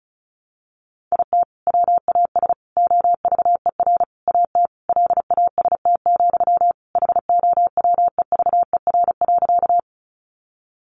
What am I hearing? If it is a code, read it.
Morse code, heard as a keyed tone, about 34 wpm, 700 Hz. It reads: IT WAS OVER AT LAST, HOWEVER.
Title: Sentences from Top 600 Words 35wpm